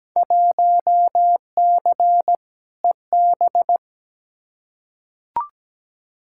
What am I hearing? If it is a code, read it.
Morse code, heard as a keyed tone, about 17 wpm, 700 Hz. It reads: 1C EB
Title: Field Day Exchanges Encore - 17wpm